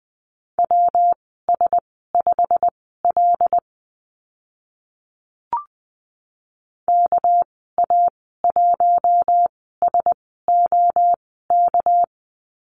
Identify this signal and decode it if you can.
Morse code, heard as a keyed tone, about 20 wpm, spaced as for 14 wpm, 700 Hz. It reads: WS5L KA1SOK
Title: 207 Lesson 33g - Letter j - Call Signs 20wpm